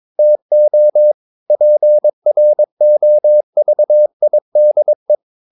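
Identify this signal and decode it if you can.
Morse code, heard as a keyed tone, about 22 wpm, 600 Hz. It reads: TO PROVIDE
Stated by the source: Morse code practice at 22 wpm, sending text from an English novel.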